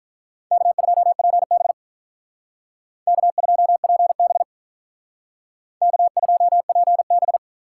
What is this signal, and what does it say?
Morse code, heard as a keyed tone, about 45 wpm, 700 Hz. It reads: K2PB K2PB K2PB